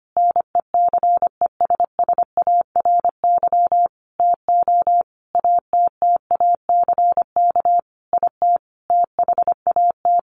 Morse code, heard as a keyed tone, about 25 wpm, 700 Hz. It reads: NECESSARY TO ATTACK IT THAT
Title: Sentences from Top 1500 Words 25wpm